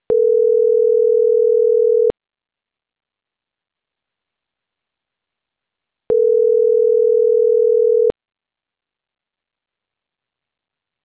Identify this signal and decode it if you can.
Ringback tone (North America), 440 + 480 Hz.